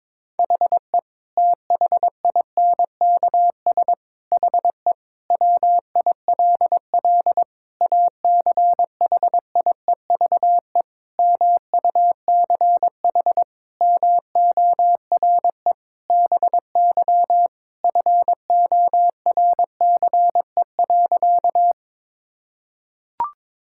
Morse code, heard as a keyed tone, about 22 wpm, 700 Hz. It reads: HE THINKS HE WILL ACHIEVE MUCH MORE BY FORCE.
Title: Sentences from Top 1000 Words - Review Part 6 - 22wpm